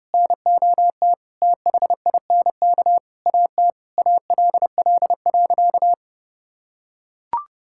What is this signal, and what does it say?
Morse code, heard as a keyed tone, about 30 wpm, 700 Hz. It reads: NOT THINK AT ALL.
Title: Sentences from Top 1600 Words 30wpm